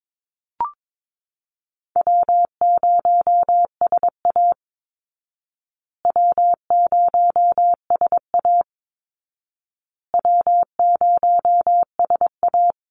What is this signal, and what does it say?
Morse code, heard as a keyed tone, about 22 wpm, 700 Hz. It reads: W0SA W0SA W0SA
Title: Easy Call Signs Repeated 3x - 22wpm